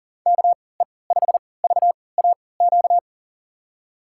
Morse code, heard as a keyed tone, about 37 wpm, spaced as for 22 wpm, 700 Hz. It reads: KE5VAQ
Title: USA Call Signs - 6 Characters - 3x Character Spacing - 40wpm